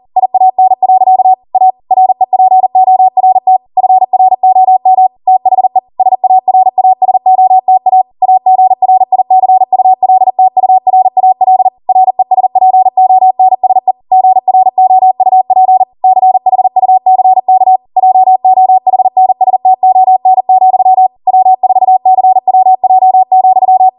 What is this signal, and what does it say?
Morse code, heard as a keyed tone, about 40 wpm, 750 Hz. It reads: IAN. A REPORT FROM THE SARASOTA AGRICULTURAL RESPONSE GROUP CHUCK JOHNSTON, W4CWJ,